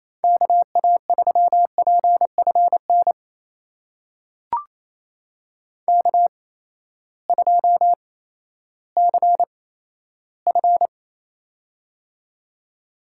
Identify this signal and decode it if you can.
Morse code, heard as a keyed tone, about 28 wpm, spaced as for 14 wpm, 700 Hz. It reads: KA3PFN K2CF
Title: USA Call Signs - 8x Character Spacing - 28wpm